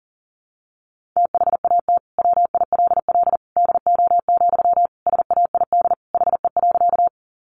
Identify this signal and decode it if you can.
Morse code, heard as a keyed tone, about 40 wpm, 700 Hz. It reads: THAT WILL DO, SAID HE.
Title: Sentences from Top 100 Words 40wpm